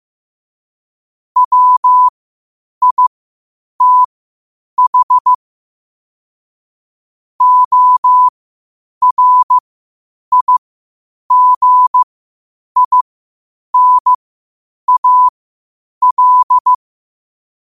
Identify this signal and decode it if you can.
Morse code, heard as a keyed tone, about 15 wpm, spaced as for 8 wpm, 1000 Hz. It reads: WITH ORIGINAL
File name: qotdfiles / 8wpm